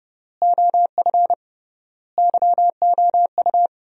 Morse code, heard as a keyed tone, about 30 wpm, 700 Hz. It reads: OF YOU
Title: 3x Word-Spacing - Sentences from Top 300 Words - 30wpm